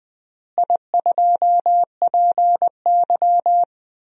Morse code, heard as a keyed tone, about 21 wpm, 700 Hz. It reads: I2PY
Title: DXCC-100 Call Signs - 4 Characters - Repeated 3x - 20wpm